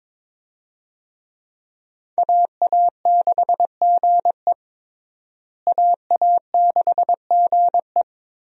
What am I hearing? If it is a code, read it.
Morse code, heard as a keyed tone, about 21 wpm, 700 Hz. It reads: AA6GE AA6GE
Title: USA Call Signs - 5 Characters - Repeated 3x - 22wpm